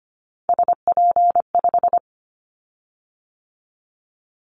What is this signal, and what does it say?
Morse code, heard as a keyed tone, about 25 wpm, 700 Hz. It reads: SP5
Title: Callsign Prefixes for Top 25 DXCC - 25wpm